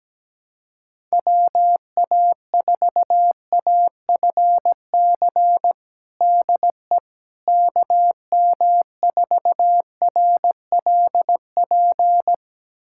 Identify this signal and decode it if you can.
Morse code, heard as a keyed tone, about 17 wpm, 700 Hz. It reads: WA4AFC DE KM4RLP
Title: USA Callsign DE Callsign - 6 Characters - 17wpm